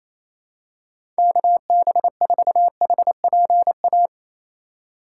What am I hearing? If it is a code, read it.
Morse code, heard as a keyed tone, about 28 wpm, 700 Hz. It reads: KB4HPA
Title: USA Call Signs - 3x Character Spacing - 28wpm